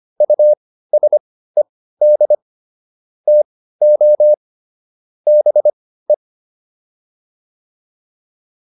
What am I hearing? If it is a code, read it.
Morse code, heard as a keyed tone, about 25 wpm, spaced as for 15 wpm, 600 Hz. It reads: USED TO BE
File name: WD205 15